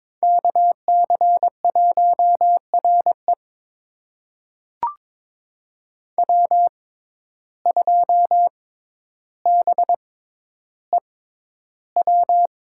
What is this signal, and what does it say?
Morse code, heard as a keyed tone, about 21 wpm, spaced as for 11 wpm, 700 Hz. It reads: KC1RE W2BEW
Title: USA Call Signs - 5 Characters - 6x Character Spacing - 22wpm